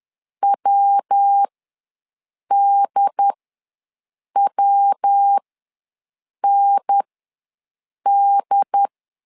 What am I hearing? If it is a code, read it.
Morse code, heard as a keyed tone, about 10 wpm, spaced as for 6 wpm, 800 Hz. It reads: WDWND
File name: LEZIONI / 3 MWRO / 3 MW ws 10wpm 2213